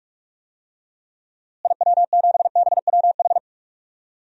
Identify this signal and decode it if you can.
Morse code, heard as a keyed tone, about 46 wpm, 700 Hz. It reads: IW7BWH
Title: DXCC-100 Call Signs - 6 Characters - Repeated 2x - 45wpm